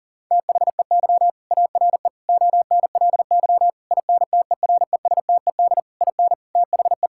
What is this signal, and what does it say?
Morse code, heard as a keyed tone, about 40 wpm, 700 Hz. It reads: THEY ARE ONLY INTERESTED IN THE
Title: Sentences from Top 1000 Words - Review Part 6 - 40wpm